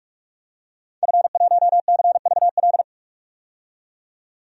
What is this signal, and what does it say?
Morse code, heard as a keyed tone, about 44 wpm, 700 Hz. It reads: F1CVL
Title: DXCC-100 Call Signs - 5 Characters - 45wpm